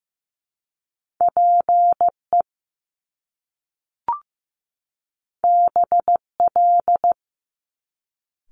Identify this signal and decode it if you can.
Morse code, heard as a keyed tone, about 15 wpm, 700 Hz. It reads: PE BL
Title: Top Two Letter Combinations in Words 15wpm